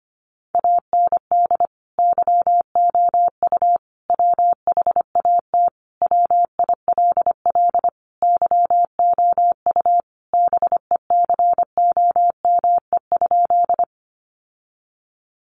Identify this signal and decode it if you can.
Morse code, heard as a keyed tone, about 25 wpm, 700 Hz. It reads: AND YOU WHAT WILL YOU BECOME?
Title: Sentences from Top 600 Words 25wpm